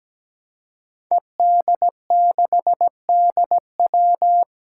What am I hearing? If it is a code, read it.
Morse code, heard as a keyed tone, about 17 wpm, 700 Hz. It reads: ED6DW
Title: Five Challenging Letter-Number Combinations 17wpm